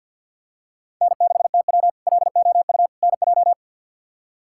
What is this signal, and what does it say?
Morse code, heard as a keyed tone, about 51 wpm, 700 Hz. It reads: N6TW LOU NJ
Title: K1USN SST QSO Practice - 50wpm